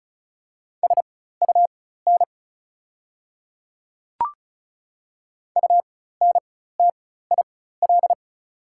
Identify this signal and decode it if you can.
Morse code, heard as a keyed tone, about 34 wpm, spaced as for 16 wpm, 700 Hz. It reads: SUN UNTIL
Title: 4x Character Spacing - Top 300 Words - 35wpm